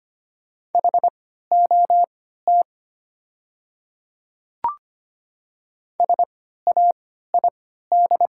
Morse code, heard as a keyed tone, about 25 wpm, spaced as for 14 wpm, 700 Hz. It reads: HOT SAID
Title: 3x Character Spacing - Top 100 Words - 25wpm